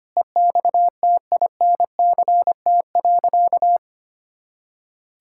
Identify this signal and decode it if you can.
Morse code, heard as a keyed tone, about 25 wpm, 700 Hz. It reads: EXTINCT.
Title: Pangram Challenge v2 - 25wpm